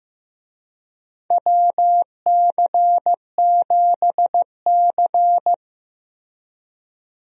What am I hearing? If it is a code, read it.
Morse code, heard as a keyed tone, about 15 wpm, 700 Hz. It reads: WC7C
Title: USA Call Signs - 4 Characters - Repeated 3x - 15wpm